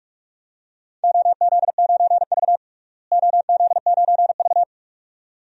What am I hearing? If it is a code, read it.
Morse code, heard as a keyed tone, about 46 wpm, 700 Hz. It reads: OZ9V OZ9V
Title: DXCC-100 Call Signs - 4 Characters - Repeated 2x - 45wpm